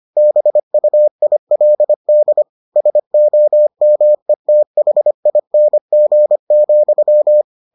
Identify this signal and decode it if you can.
Morse code, heard as a keyed tone, about 25 wpm, 600 Hz. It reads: BUILD SOMETHING,